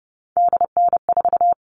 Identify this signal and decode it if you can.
Morse code, heard as a keyed tone, about 30 wpm, 700 Hz. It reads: DN4
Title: All Cut Numbers - 3-Digit 30wpm